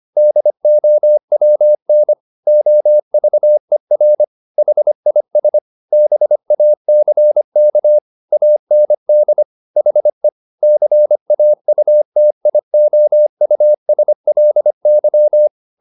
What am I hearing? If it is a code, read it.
Morse code, heard as a keyed tone, about 25 wpm, 600 Hz. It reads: DOWN OVER HIS BACK AND HE CAUTIOUSLY
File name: the old man and the sea part 1 25wpm